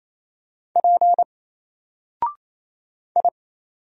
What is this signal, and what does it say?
Morse code, heard as a keyed tone, about 27 wpm, 700 Hz. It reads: P I
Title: Single Letter-Number Alt 28wpm